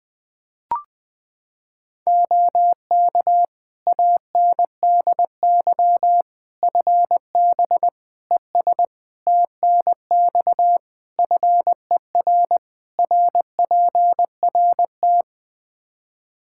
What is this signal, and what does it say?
Morse code, heard as a keyed tone, about 20 wpm, 700 Hz. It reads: OK ANDY FB ES TNX FER RPRT
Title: Practice QSO - Exchange 2 - Info - 20wpm